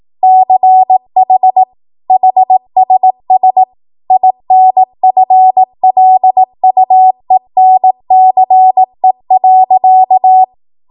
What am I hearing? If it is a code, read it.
Morse code, heard as a keyed tone, about 18 wpm, 750 Hz. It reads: CH HSS INFLUENCE.
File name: Morse / 18WPM